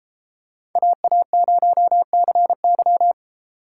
Morse code, heard as a keyed tone, about 33 wpm, 700 Hz. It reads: AA0CY